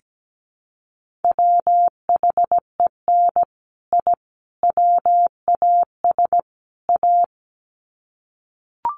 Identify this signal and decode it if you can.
Morse code, heard as a keyed tone, about 17 wpm, 700 Hz. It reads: WHEN I WAS A
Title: Sets of 4 Words 17wpm